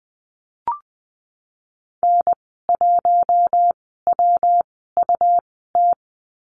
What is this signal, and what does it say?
Morse code, heard as a keyed tone, about 20 wpm, spaced as for 15 wpm, 700 Hz. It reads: N1WUT
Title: 122 Lesson 20g - Number 3 - Call Signs 20wpm